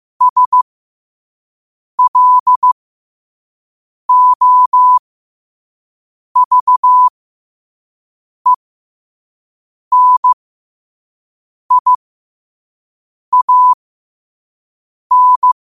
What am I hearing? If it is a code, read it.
Morse code, heard as a keyed tone, about 15 wpm, spaced as for 5 wpm, 1000 Hz. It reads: SLOVENIAN
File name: qotdfiles / 5wpm